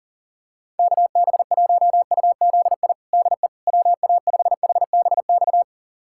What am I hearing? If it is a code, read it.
Morse code, heard as a keyed tone, about 40 wpm, 700 Hz. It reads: KB1UZI DE WA5HBX
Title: USA Callsign DE Callsign 40wpm